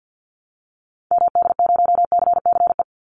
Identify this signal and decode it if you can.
Morse code, heard as a keyed tone, about 50 wpm, 700 Hz. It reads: MD0CCE
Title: Call Signs Repeated 2x - 50wpm